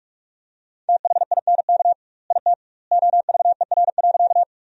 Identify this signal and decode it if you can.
Morse code, heard as a keyed tone, about 41 wpm, 700 Hz. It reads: THINK IT OVER.
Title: 2x Word-Spacing - Sentences from Top 200 Words - 45wpm